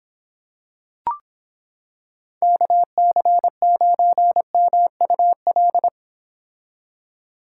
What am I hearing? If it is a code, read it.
Morse code, heard as a keyed tone, about 26 wpm, 700 Hz. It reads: KC9MUL